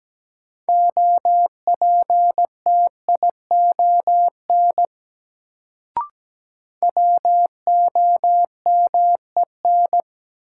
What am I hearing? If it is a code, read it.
Morse code, heard as a keyed tone, about 17 wpm, 700 Hz. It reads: OPTION WOMEN
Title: Sentences from Top 1300 Words - New Word Drill Alt - 17wpm